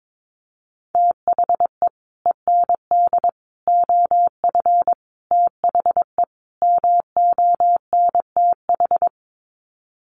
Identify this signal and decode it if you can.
Morse code, heard as a keyed tone, about 22 wpm, 700 Hz. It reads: THE END OF THE MONTH
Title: Sets of 5 Words Encore 22wpm